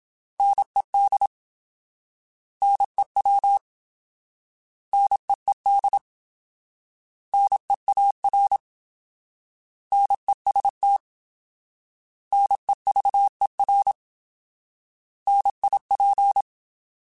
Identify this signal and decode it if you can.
Morse code, heard as a keyed tone, about 27 wpm, 800 Hz. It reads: NED NEW NEED NEAR NEST NEVER NIP
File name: Alphabetical / Alphabetical mp3s / Ns @25